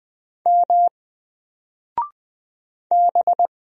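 Morse code, heard as a keyed tone, about 20 wpm, 700 Hz. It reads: M B
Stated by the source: Single letter and number Morse drill at 20 wpm, from N4LCD.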